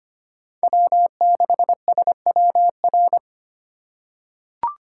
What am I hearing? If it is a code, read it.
Morse code, heard as a keyed tone, about 25 wpm, 700 Hz. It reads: W6SWR
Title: USA Call Signs - 4x Character Spacing - 25wpm